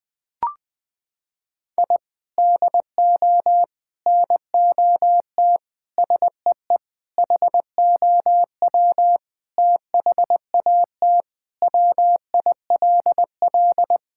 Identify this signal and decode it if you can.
Morse code, heard as a keyed tone, about 21 wpm, 700 Hz. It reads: I DO NOT SEE HOW THAT WILL
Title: Sentences with 5-Word Phrases - 20wpm